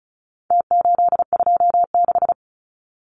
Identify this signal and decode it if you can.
Morse code, heard as a keyed tone, about 34 wpm, 700 Hz. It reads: T826
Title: Numbers - Contest Style - 4-Digit 35wpm